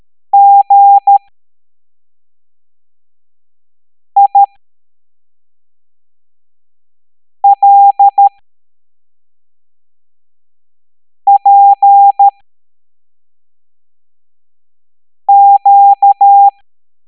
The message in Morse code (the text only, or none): GILPQ